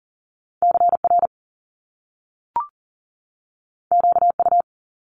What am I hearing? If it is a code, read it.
Morse code, heard as a keyed tone, about 40 wpm, 700 Hz. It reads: CR QU
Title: Top Two Letter Combinations in Words 40wpm